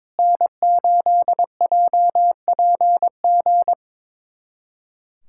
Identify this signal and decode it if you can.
Morse code, heard as a keyed tone, about 21 wpm, 700 Hz. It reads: N8JPG
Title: USA Callsign DE Callsign - 5 Characters - 22wpm